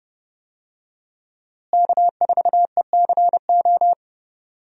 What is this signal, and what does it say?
Morse code, heard as a keyed tone, about 30 wpm, 700 Hz. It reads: K4ECO
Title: USA Call Signs - 5 Characters - Repeated 3x - 30wpm